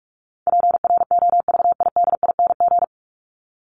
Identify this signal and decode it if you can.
Morse code, heard as a keyed tone, about 44 wpm, 700 Hz. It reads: PROVIDING
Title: ING Suffix - Top 100 Words with Base Words 45wpm